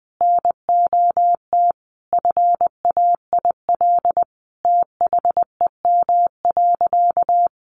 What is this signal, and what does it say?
Morse code, heard as a keyed tone, about 20 wpm, 700 Hz. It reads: NOT FAIL THEM.
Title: Sentences from Top 1100 Words 20wpm